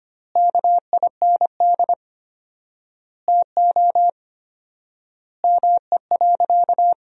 Morse code, heard as a keyed tone, about 25 wpm, 700 Hz. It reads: KIND TO ME.
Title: 4x Word-Spacing - Sentences from Top 200 Words - 25wpm